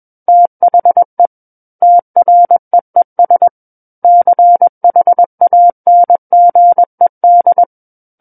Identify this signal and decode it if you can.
Morse code, heard as a keyed tone, about 21 wpm, 700 Hz. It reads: THE TREES CHANGED
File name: Book One / 12-Chapter 12 - 21 WPM